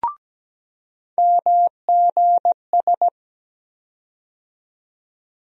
Morse code, heard as a keyed tone, about 17 wpm, 700 Hz. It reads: MGS
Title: USA Call Sign Postfixes - 3 Letters 17wpm